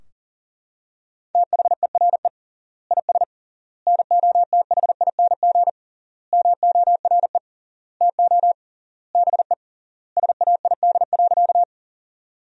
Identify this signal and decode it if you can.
Morse code, heard as a keyed tone, about 40 wpm, 700 Hz. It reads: THERE IS NOTHING MORE TO BE SAID.
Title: 3x Word-Spacing - Sentences from Top 500 Words - 40wpm